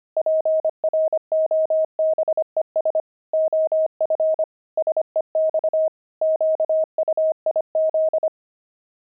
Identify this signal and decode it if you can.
Morse code, heard as a keyed tone, about 25 wpm, 630 Hz. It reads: PROBES OF SEX QUIZ
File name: pangram-fs-25-cs-25-wived